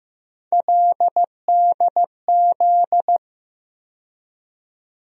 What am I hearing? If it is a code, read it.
Morse code, heard as a keyed tone, about 15 wpm, 700 Hz. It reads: LDZ
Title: USA Call Sign Postfixes - 3 Letters 15wpm